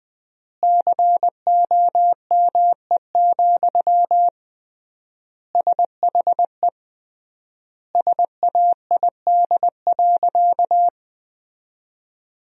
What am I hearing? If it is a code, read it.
Morse code, heard as a keyed tone, about 19 wpm, 700 Hz. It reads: COME, SHE SAID.
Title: 3x Word-Spacing - Sentences from Top 100 Words - 20wpm